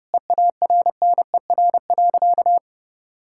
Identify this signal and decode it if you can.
Morse code, heard as a keyed tone, about 30 wpm, 700 Hz. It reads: EARNER.
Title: Pangram Challenge v2 - 30wpm